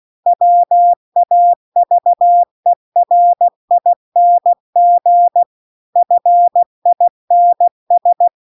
Morse code, heard as a keyed tone, about 16 wpm, 700 Hz. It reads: WAVERING FINS
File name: CW-The-Big-Twoheaded-River-1-16WPM